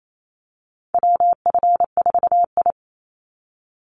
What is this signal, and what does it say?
Morse code, heard as a keyed tone, about 28 wpm, 700 Hz. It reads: WF4I
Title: NA SOTA Chaser Callsigns 28wpm